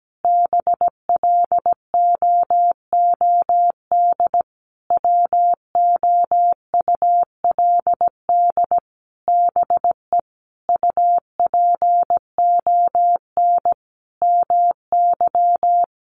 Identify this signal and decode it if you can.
Morse code, heard as a keyed tone, about 17 wpm, 700 Hz. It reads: BLOOD WOULD BE UPON MY
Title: Sentences from Top 1100 Words 17wpm